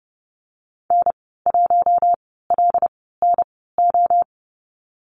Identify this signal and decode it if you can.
Morse code, heard as a keyed tone, about 30 wpm, spaced as for 17 wpm, 700 Hz. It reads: N1LNO
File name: Lesson 15g - Number 5 - Call Signs 30wpm